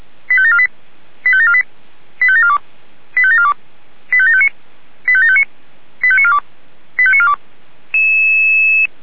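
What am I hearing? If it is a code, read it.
Five-tone selective call (ZVEI), heard as quick runs of five tones, tones of about 70 ms. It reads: 85748 85748 85742 85742 85759 85759 85942 85942
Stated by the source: ZVEI five-tone selective call received by SDR.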